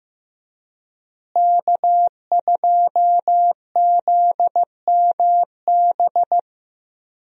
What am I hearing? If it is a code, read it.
Morse code, heard as a keyed tone, about 15 wpm, 700 Hz. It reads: K2ZMB
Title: USA Call Signs - 5 Characters - Repeated 3x - 15wpm